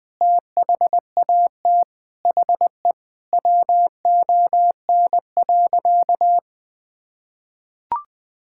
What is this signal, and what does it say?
Morse code, heard as a keyed tone, about 20 wpm, 700 Hz. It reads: THAT HE WON.
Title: Sentences with 5-Word Phrases - Spoken Beginning - 20wpm